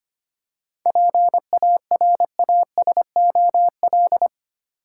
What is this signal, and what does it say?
Morse code, heard as a keyed tone, about 25 wpm, 700 Hz. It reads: PARASOL
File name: Easy Words - Speed-Racing with Hints - 25wpm